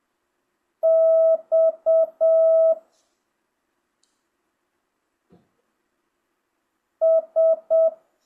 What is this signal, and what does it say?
Morse code, heard as a keyed tone, about 7 wpm, 1300 Hz. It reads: X S